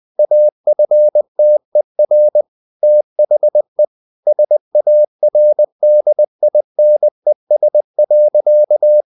Morse code, heard as a keyed tone, about 20 wpm, 600 Hz. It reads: AFTER THE SARDINES.
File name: the old man and the sea / the old man and the sea part 1 20wpm